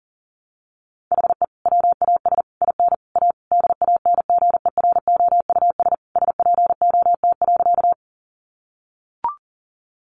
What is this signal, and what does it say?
Morse code, heard as a keyed tone, about 40 wpm, 700 Hz. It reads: HE WAS IN A DANGEROUS SPOT.
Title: Sentences from Top 1200 Words 40wpm